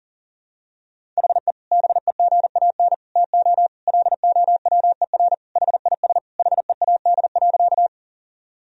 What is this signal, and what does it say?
Morse code, heard as a keyed tone, about 37 wpm, 700 Hz. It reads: HE BEGAN TO LOWER HIS HEAD.
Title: Sentences from Top 1000 Words - Review Part 4 - 40wpm